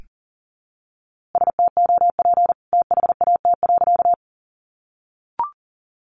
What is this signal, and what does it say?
Morse code, heard as a keyed tone, about 40 wpm, 700 Hz. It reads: STOP THAT.
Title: A Sound Of Thunder 40wpm